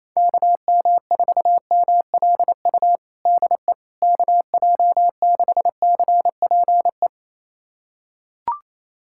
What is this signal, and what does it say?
Morse code, heard as a keyed tone, about 28 wpm, 700 Hz. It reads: KM4MLU DE KJ6CPE
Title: USA Callsign DE Callsign 28wpm